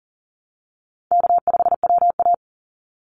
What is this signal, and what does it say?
Morse code, heard as a keyed tone, about 40 wpm, 700 Hz. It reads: K5WA
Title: Easy Call Signs Repeated 3x - 40wpm